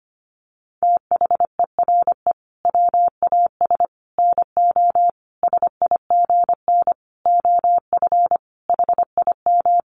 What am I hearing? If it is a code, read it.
Morse code, heard as a keyed tone, about 25 wpm, 700 Hz. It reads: THERE WAS NO SIGN OF HIM
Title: Sentences from Top 1000 Words 25wpm